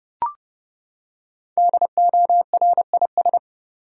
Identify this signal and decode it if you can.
Morse code, heard as a keyed tone, about 32 wpm, 700 Hz. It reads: DORIS
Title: USA Common Names - 30wpm